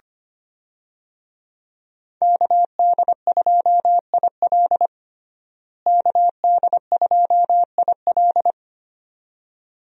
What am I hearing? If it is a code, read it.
Morse code, heard as a keyed tone, about 25 wpm, 700 Hz. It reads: KD2IL KD2IL
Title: USA Call Signs - 5 Characters - Repeated 2x - 25wpm